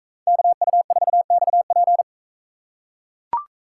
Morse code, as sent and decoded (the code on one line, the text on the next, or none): -.- ..- ....- -..- .--.
KU4XP